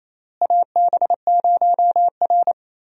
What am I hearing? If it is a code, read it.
Morse code, heard as a keyed tone, about 28 wpm, 700 Hz. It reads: AB0R
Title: USA Call Signs - 4 Characters - Repeated 2x - 28wpm